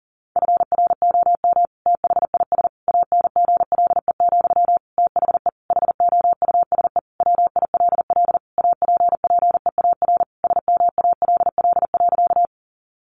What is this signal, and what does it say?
Morse code, heard as a keyed tone, about 37 wpm, 700 Hz. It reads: FROM THIS ANGLE, THE HOUSE WILL APPEAR SMALL.
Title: Sentences from Top 1500 Words 40wpm